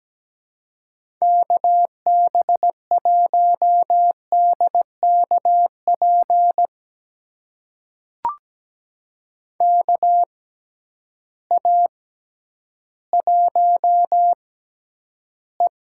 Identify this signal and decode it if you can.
Morse code, heard as a keyed tone, about 17 wpm, spaced as for 10 wpm, 700 Hz. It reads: KB1DKP KA1E
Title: USA Call Signs - 6 Characters - 6x Character Spacing - 17wpm